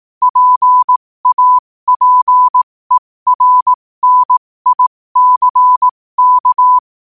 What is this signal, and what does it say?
Morse code, heard as a keyed tone, about 18 wpm, spaced as for 15 wpm, 1000 Hz. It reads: PAPERNICK
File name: qotdfiles / 15wpm